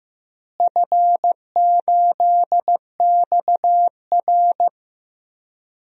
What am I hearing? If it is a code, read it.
Morse code, heard as a keyed tone, about 15 wpm, 700 Hz. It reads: F8XR
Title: DXCC-100 Call Signs - 4 Characters - Repeated 2x - 15wpm